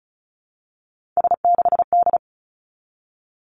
Morse code, heard as a keyed tone, about 36 wpm, 700 Hz. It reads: S6D
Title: Three Challenging Letter-Number Combinations 35wpm